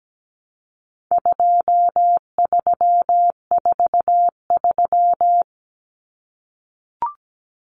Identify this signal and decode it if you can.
Morse code, heard as a keyed tone, about 17 wpm, 700 Hz. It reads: 2343
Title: Numbers - 4-Digit 17wpm